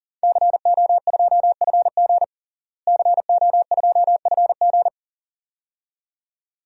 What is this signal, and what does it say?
Morse code, heard as a keyed tone, about 40 wpm, 700 Hz. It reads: CO2FG CO2FG
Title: DXCC-100 Call Signs - 5 Characters - Repeated 2x - 40wpm